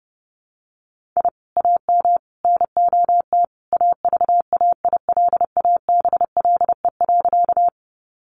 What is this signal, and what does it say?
Morse code, heard as a keyed tone, about 30 wpm, 700 Hz. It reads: I AM NOT AVAILABLE.
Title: Sentences from Top 600 Words 30wpm